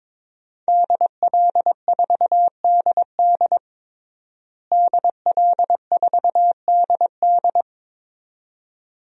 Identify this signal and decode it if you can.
Morse code, heard as a keyed tone, about 22 wpm, 700 Hz. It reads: DL4DD DL4DD
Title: DXCC-100 Call Signs - 5 Characters - Repeated 2x - 22wpm